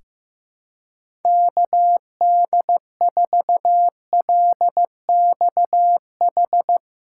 Morse code, heard as a keyed tone, about 15 wpm, 700 Hz. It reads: KD4LXH
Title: USA Callsign DE Callsign 15wpm